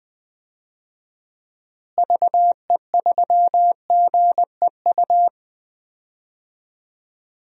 Morse code, heard as a keyed tone, about 20 wpm, 700 Hz. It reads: VE3GEU
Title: DXCC-100 Call Signs - 6 Characters - Repeated 3x - 20wpm